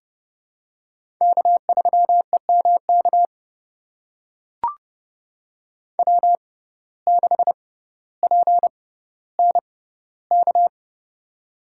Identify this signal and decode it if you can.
Morse code, heard as a keyed tone, about 31 wpm, spaced as for 15 wpm, 700 Hz. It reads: K3EMK W6PNK
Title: USA Call Signs - 5 Characters - 6x Character Spacing - 30wpm